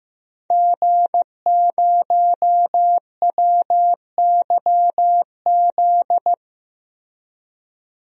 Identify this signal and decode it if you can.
Morse code, heard as a keyed tone, about 15 wpm, 700 Hz. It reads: G0WYZ
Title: DXCC-100 Call Signs - 5 Characters - Repeated 3x - 15wpm